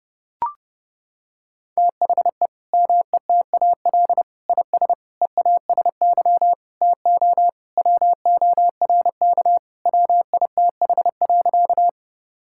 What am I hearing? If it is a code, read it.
Morse code, heard as a keyed tone, about 30 wpm, 700 Hz. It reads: THE METAL IS EASY TO WORK WITH.
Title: Sentences from Top 1000 Words - Review Part 6 - 30wpm